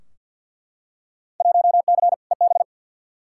Morse code, heard as a keyed tone, about 50 wpm, 700 Hz. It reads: 1C EB
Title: Field Day Exchanges Encore - 50wpm